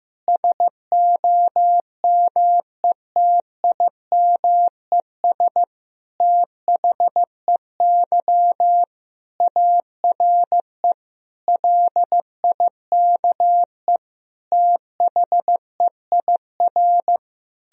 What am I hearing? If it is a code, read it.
Morse code, heard as a keyed tone, about 15 wpm, 700 Hz. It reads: SOMETIMES THEY ARE LIKE THEIR
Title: Sentences from Top 1000 Words - Review Part 3 - 15wpm